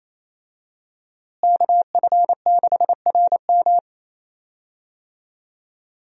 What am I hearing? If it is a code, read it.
Morse code, heard as a keyed tone, about 28 wpm, 700 Hz. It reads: KF6RM
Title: USA Call Signs - 5 Characters - Repeated 3x - 28wpm